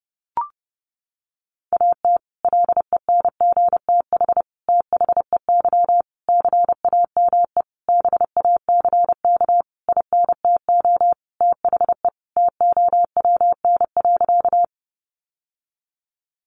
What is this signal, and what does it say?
Morse code, heard as a keyed tone, about 28 wpm, 700 Hz. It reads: AT LENGTH THEY CAME BACK INTO THE TOWN.
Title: Sentences from Top 1000 Words 30wpm